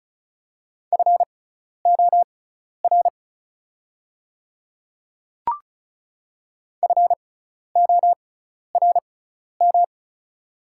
Morse code, heard as a keyed tone, about 36 wpm, spaced as for 12 wpm, 700 Hz. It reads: FOR FORM
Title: 6x Character Spacing - Top 200 Words - 35wpm